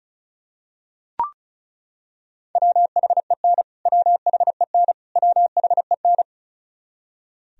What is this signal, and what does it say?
Morse code, heard as a keyed tone, about 36 wpm, 700 Hz. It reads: WHEN WHEN WHEN
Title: Top 100 Words Repeated 35wpm